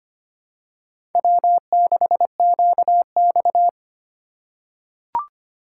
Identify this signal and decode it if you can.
Morse code, heard as a keyed tone, about 25 wpm, 700 Hz. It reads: W6QX
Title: USA Call Signs - 4 Characters - 2x Character Spacing - 25wpm